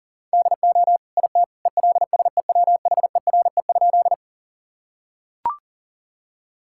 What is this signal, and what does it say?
Morse code, heard as a keyed tone, about 40 wpm, 700 Hz. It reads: DO IT ELSEWHERE?
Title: Sentences from Top 1600 Words 40wpm